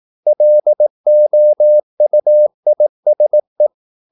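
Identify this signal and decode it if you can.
Morse code, heard as a keyed tone, about 18 wpm, 600 Hz. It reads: LOUISE
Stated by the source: Slow (QRS) Morse practice at 18 wpm, CW med Gnister.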